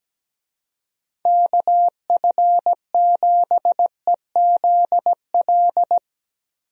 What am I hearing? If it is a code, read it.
Morse code, heard as a keyed tone, about 17 wpm, 700 Hz. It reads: KF7EZL
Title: USA Call Signs - 6 Characters - Repeated 2x - 17wpm